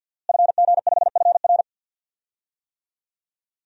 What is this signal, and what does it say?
Morse code, heard as a keyed tone, about 51 wpm, 700 Hz. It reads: FG5FR
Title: DXCC-100 Call Signs - 5 Characters - Repeated 2x - 50wpm